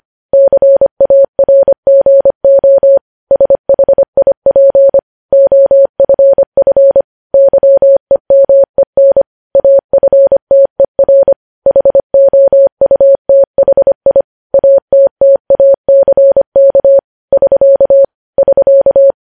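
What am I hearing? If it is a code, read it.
Morse code, heard as a keyed tone, about 25 wpm, 550 Hz. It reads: CARGO SHIP OFF YEMEN AFTER HOUTHI ATTACK <SK> <SK>
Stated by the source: Morse code news headlines sent at 25 wpm.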